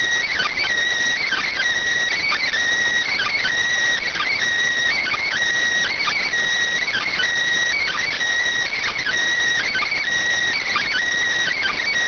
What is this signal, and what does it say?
Emergency Alert System (SAME) tones (header bursts), heard as harsh buzzing bursts. It is heard from 1.2 to 11.0 s.